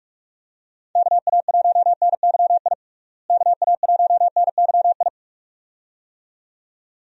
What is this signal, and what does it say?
Morse code, heard as a keyed tone, about 41 wpm, 700 Hz. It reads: KA1NYI KA1NYI
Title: USA Call Signs - 6 Characters - Repeated 2x - 45wpm